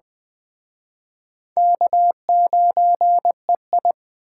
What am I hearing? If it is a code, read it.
Morse code, heard as a keyed tone, about 20 wpm, 700 Hz. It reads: K9EI